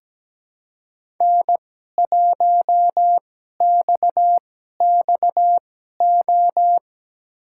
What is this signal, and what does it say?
Morse code, heard as a keyed tone, about 17 wpm, spaced as for 12 wpm, 700 Hz. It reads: N1XXO
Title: USA Call Signs - 5 Characters - 2x Character Spacing - 17wpm